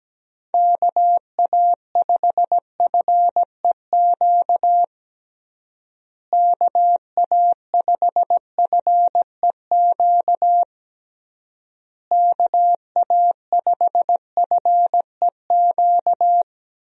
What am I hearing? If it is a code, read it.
Morse code, heard as a keyed tone, about 17 wpm, 700 Hz. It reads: KA5FEQ KA5FEQ KA5FEQ
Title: USA Call Signs - 6 Characters - Repeated 3x - 17wpm